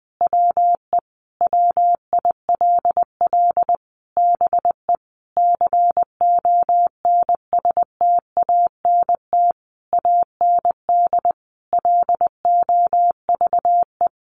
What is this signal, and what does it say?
Morse code, heard as a keyed tone, about 20 wpm, 700 Hz. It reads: WE WILL BE CONSTANT AND LOVE
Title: Sentences from Top 1500 Words 20wpm